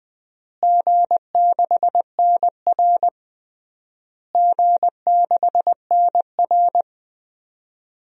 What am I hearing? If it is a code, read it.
Morse code, heard as a keyed tone, about 20 wpm, 700 Hz. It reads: G6NR G6NR
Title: DXCC-100 Call Signs - 4 Characters - Repeated 3x - 20wpm